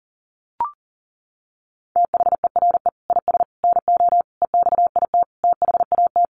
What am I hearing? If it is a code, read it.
Morse code, heard as a keyed tone, about 40 wpm, 700 Hz. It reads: THERE IS NO EXIT THAT
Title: Sentences from Top 1100 Words 40wpm